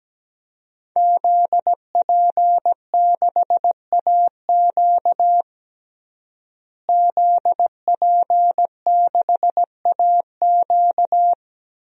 Morse code, heard as a keyed tone, about 17 wpm, 700 Hz. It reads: ZP6AQ ZP6AQ
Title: DXCC-100 Call Signs - 5 Characters - Repeated 2x - 17wpm